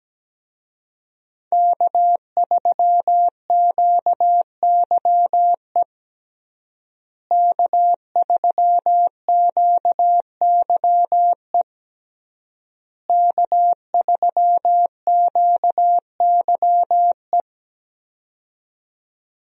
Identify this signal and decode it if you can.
Morse code, heard as a keyed tone, about 17 wpm, 700 Hz. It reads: K3QYE K3QYE K3QYE
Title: USA Call Signs - 5 Characters - Repeated 3x - 17wpm